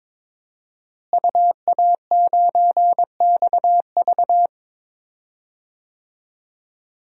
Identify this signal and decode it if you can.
Morse code, heard as a keyed tone, about 22 wpm, 700 Hz. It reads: UA9XV
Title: DXCC-100 Call Signs - 5 Characters - Repeated 3x - 22wpm